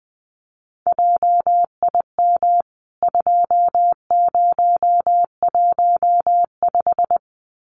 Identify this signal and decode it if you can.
Morse code, heard as a keyed tone, about 20 wpm, 700 Hz. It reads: JIM 2015
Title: CWT Exchange 20wpm